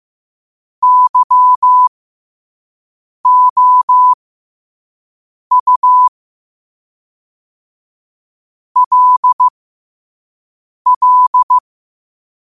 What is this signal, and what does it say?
Morse code, heard as a keyed tone, about 15 wpm, spaced as for 5 wpm, 1000 Hz. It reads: YOU LL
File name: qotdfiles / 5wpm